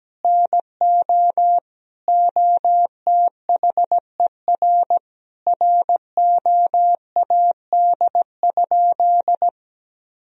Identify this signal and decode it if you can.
Morse code, heard as a keyed tone, about 17 wpm, 700 Hz. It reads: NO OTHER ROAD?
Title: Sentences from Top 1000 Words - Review Part 2 - 17wpm